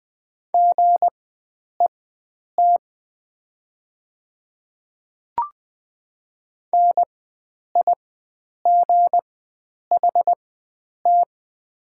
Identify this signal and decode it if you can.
Morse code, heard as a keyed tone, about 20 wpm, spaced as for 9 wpm, 700 Hz. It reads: GET NIGHT
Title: 4x Character Spacing - Top 300 Words - 20wpm